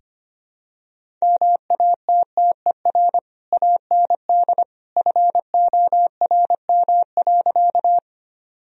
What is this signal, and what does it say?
Morse code, heard as a keyed tone, about 25 wpm, 700 Hz. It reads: MATTER AND FORM.
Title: Sentences from Top 1000 Words - Review Part 1 - 25wpm